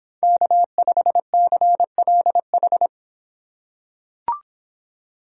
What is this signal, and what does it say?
Morse code, heard as a keyed tone, about 26 wpm, 700 Hz. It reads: K5CLH